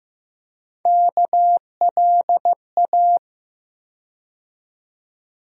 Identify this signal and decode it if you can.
Morse code, heard as a keyed tone, about 15 wpm, 700 Hz. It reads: KLA
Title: USA Call Sign Postfixes - 3 Letters 15wpm